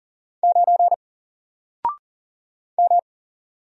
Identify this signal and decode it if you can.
Morse code, heard as a keyed tone, about 40 wpm, 700 Hz. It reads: 9 M
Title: Single Letter-Number Alt 40wpm